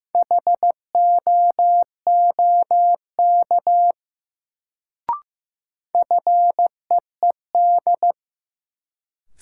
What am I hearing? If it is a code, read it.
Morse code, heard as a keyed tone, about 15 wpm, 700 Hz. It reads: HOOK FEED
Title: Sentences from Top 1600 Words - New Word Drill Alt - 15wpm